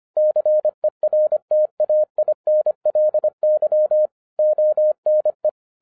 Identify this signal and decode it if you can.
Morse code, heard as a keyed tone, about 25 wpm, 610 Hz. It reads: CERTAINLY ONE
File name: FIRST FISTS CONTACT-25wpm